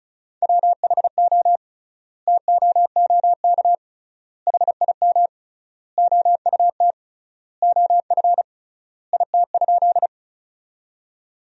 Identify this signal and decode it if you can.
Morse code, heard as a keyed tone, about 34 wpm, 700 Hz. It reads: WHO TOOK HIM OUT OF IT?
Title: 3x Word-Spacing - Sentences from Top 300 Words - 35wpm